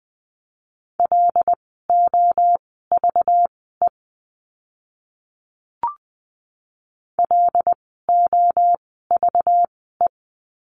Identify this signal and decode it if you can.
Morse code, heard as a keyed tone, about 20 wpm, spaced as for 14 wpm, 700 Hz. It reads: LOVE LOVE
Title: 192 Lesson 31f - Letter v - Words 20wpm